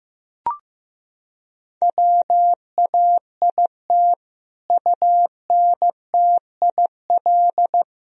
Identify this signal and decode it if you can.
Morse code, heard as a keyed tone, about 15 wpm, 700 Hz. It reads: WAIT UNTIL
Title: Sentences from Top 1000 Words - Review Part 1 - 15wpm